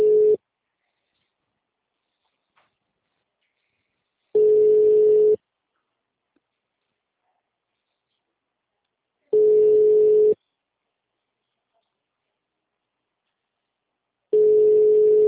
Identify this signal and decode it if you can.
Ringback tone (Europe, 425 Hz).
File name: out- -2064